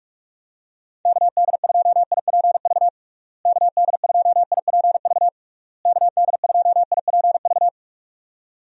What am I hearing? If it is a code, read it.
Morse code, heard as a keyed tone, about 46 wpm, 700 Hz. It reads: KD2IPV KD2IPV KD2IPV
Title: USA Call Signs - 6 Characters - Repeated 3x - 45wpm